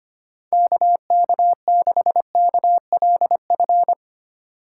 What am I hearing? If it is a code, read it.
Morse code, heard as a keyed tone, about 25 wpm, 700 Hz. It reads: KK6KLF
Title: USA Call Signs - 6 Characters - 3x Character Spacing - 25wpm